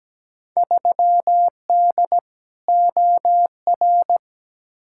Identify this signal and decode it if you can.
Morse code, heard as a keyed tone, about 17 wpm, 700 Hz. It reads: 3D OR
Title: Field Day Exchanges Encore - 17wpm